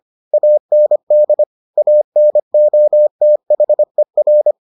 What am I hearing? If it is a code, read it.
Morse code, heard as a keyed tone, about 25 wpm, 600 Hz. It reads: AND ANOTHER